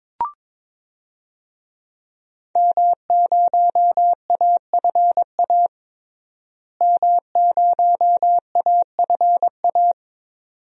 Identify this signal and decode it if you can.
Morse code, heard as a keyed tone, about 22 wpm, 700 Hz. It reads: M0AFA M0AFA
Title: DXCC-100 Call Signs - 5 Characters - Repeated 2x - 22wpm